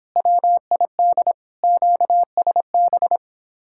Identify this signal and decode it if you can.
Morse code, heard as a keyed tone, about 24 wpm, 700 Hz. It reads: WID QSB
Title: Speed-Racing Common QSO Phrases - 22wpm